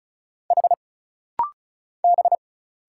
Morse code, heard as a keyed tone, about 36 wpm, 700 Hz. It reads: H B
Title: Single Letter-Number Alt - Rapid-Fire 35wpm